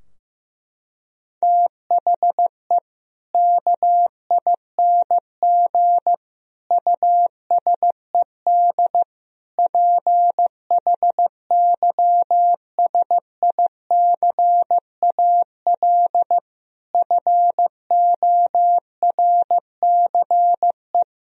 Morse code, heard as a keyed tone, about 16 wpm, 700 Hz. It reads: THE KING USED PHYSICAL FORCE
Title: Sentences from Top 1000 Words - Review Part 4 - 15wpm